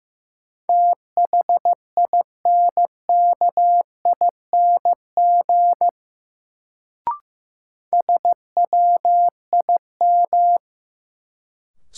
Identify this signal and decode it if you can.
Morse code, heard as a keyed tone, about 15 wpm, 700 Hz. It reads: THINKING SWIM
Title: Sentences from Top 1000 Words - Review Part 6 - New Word Drill - 15wpm